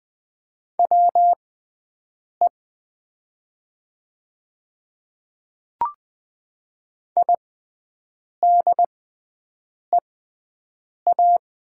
Morse code, heard as a keyed tone, about 20 wpm, spaced as for 7 wpm, 700 Hz. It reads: WE IDEA
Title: 6x Character Spacing - Top 300 Words - 20wpm